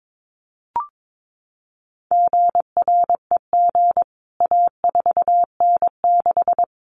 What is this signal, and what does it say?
Morse code, heard as a keyed tone, about 22 wpm, 700 Hz. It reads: GREG A4N6
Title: CWT Exchange 22wpm